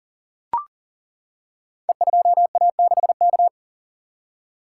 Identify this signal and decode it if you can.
Morse code, heard as a keyed tone, about 40 wpm, 700 Hz. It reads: E2A6K
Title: Code Groups - 5 Letter:Numbers - 40wpm